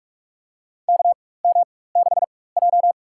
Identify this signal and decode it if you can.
Morse code, heard as a keyed tone, about 44 wpm, spaced as for 21 wpm, 700 Hz. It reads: KM6J
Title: USA Call Signs - 4x Character Spacing - 45wpm